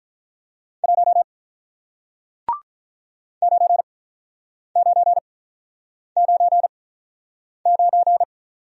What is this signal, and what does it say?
Morse code, heard as a keyed tone, about 48 wpm, spaced as for 9 wpm, 700 Hz. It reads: 1 9999